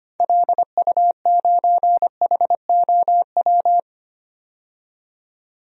Morse code, heard as a keyed tone, about 25 wpm, 700 Hz. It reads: LU9HOW
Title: DXCC-100 Call Signs - 6 Characters - 25wpm